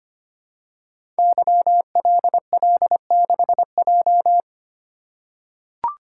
Morse code, heard as a keyed tone, about 25 wpm, 700 Hz. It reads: YLL6J
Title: Code Groups - 5 Letter:Numbers - 25wpm